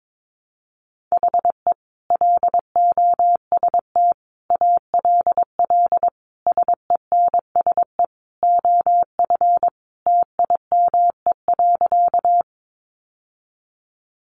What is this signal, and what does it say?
Morse code, heard as a keyed tone, about 22 wpm, 700 Hz. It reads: HE LOST ALL SENSE OF TIME.
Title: Sentences from Top 1400 Words 22wpm